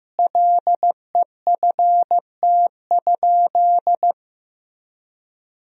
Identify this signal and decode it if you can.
Morse code, heard as a keyed tone, about 15 wpm, 700 Hz. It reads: LEFT?
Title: Sentences from Top 1000 Words - Review Part 2 - 15wpm